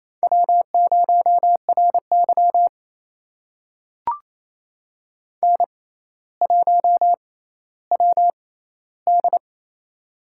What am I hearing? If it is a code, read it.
Morse code, heard as a keyed tone, about 28 wpm, spaced as for 14 wpm, 700 Hz. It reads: W0RY N1WD
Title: USA Call Signs - 4 Characters - 6x Character Spacing - 28wpm